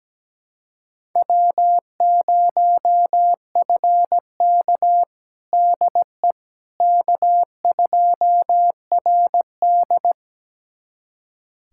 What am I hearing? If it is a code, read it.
Morse code, heard as a keyed tone, about 17 wpm, 700 Hz. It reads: W0FK DE K2RD
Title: USA Callsign DE Callsign - 4 Characters - 17wpm